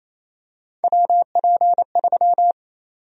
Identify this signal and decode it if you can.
Morse code, heard as a keyed tone, about 27 wpm, 700 Hz. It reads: WP3
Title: USA Call Sign Prefixes - Repeated 2x - 28wpm